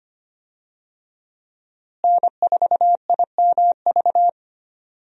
Morse code, heard as a keyed tone, about 25 wpm, 700 Hz. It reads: N4IMV
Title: USA Call Signs - 5 Characters - Repeated 2x - 25wpm